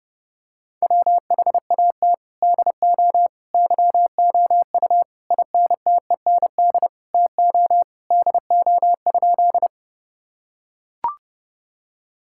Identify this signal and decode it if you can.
Morse code, heard as a keyed tone, about 30 wpm, 700 Hz. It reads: WHAT DO YOU INTEND TO DO?
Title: Sentences from Top 1600 Words 30wpm